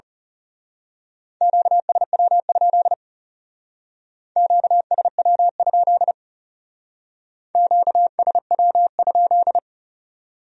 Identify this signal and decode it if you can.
Morse code, heard as a keyed tone, about 34 wpm, 700 Hz. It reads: QSW? QSW? QSW?
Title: Speed-Racing Q-Codes - 30wpm